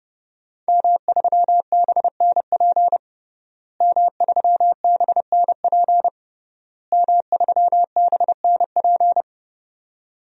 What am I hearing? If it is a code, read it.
Morse code, heard as a keyed tone, about 30 wpm, 700 Hz. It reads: M3BNP M3BNP M3BNP
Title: DXCC-100 Call Signs - 5 Characters - Repeated 3x - 30wpm